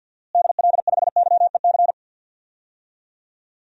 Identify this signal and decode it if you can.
Morse code, heard as a keyed tone, about 52 wpm, 700 Hz. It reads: DL5YEC
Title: DXCC-100 Call Signs - 6 Characters - Repeated 2x - 50wpm